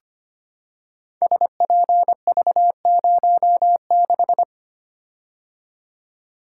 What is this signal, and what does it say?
Morse code, heard as a keyed tone, about 25 wpm, 700 Hz. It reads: SPV06
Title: Code Groups - 5 Letter:Numbers - 25wpm